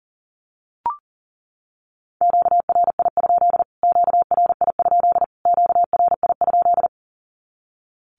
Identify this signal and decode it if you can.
Morse code, heard as a keyed tone, about 40 wpm, 700 Hz. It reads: QRI? QRI? QRI?
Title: Q-Codes Repeated 3x 40wpm